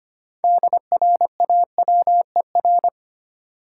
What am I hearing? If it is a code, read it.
Morse code, heard as a keyed tone, about 25 wpm, 700 Hz. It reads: DRAWER
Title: Six Letter Words - 25wpm